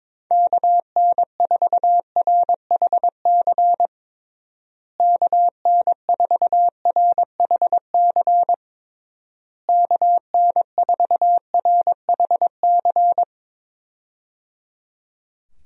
Morse code, heard as a keyed tone, about 22 wpm, 700 Hz. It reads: KN4RHC KN4RHC KN4RHC
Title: USA Call Signs - 6 Characters - Repeated 3x - 22wpm